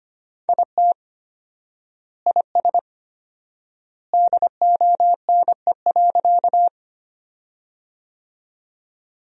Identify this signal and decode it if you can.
Morse code, heard as a keyed tone, about 25 wpm, 700 Hz. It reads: IT IS DONE.
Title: 4x Word-Spacing - Sentences from Top 500 Words - 25wpm